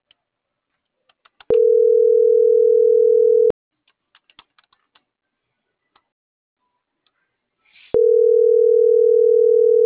Ringback tone (North America), 440 + 480 Hz.